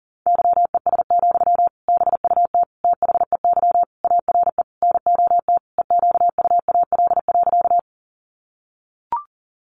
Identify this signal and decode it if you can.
Morse code, heard as a keyed tone, about 40 wpm, 700 Hz. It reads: YES, BUT THEY ARE NOT EQUAL.
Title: Sentences from Top 1400 Words 40wpm